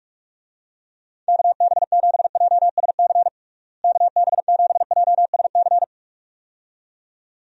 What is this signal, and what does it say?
Morse code, heard as a keyed tone, about 46 wpm, 700 Hz. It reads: KB7JSC KB7JSC
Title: USA Call Signs - 6 Characters - Repeated 2x - 45wpm